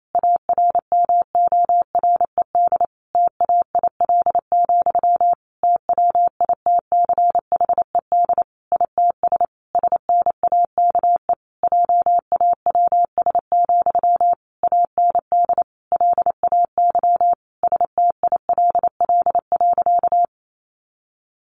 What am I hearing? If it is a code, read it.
Morse code, heard as a keyed tone, about 28 wpm, 700 Hz. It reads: ARMORED TAIL, TWITCHED ITS SNAKE JAWS, AND LAY STILL.
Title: A Sound Of Thunder 28wpm